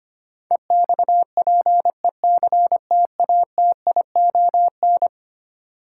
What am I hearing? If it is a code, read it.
Morse code, heard as a keyed tone, about 25 wpm, 700 Hz. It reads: EXPECTATION
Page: morse-code-ninja-practice-25wpm